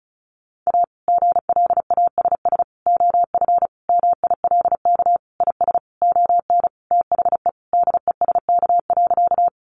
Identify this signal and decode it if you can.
Morse code, heard as a keyed tone, about 34 wpm, 700 Hz. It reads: A GLASS OF MILK IS ON THE DESK.
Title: Additional Easy Phrases 35wpm